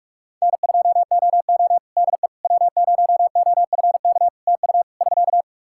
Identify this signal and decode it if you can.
Morse code, heard as a keyed tone, about 46 wpm, 700 Hz. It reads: N2OO DE W0OFK TU <SK>
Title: K1USN SST QSO Practice 45wpm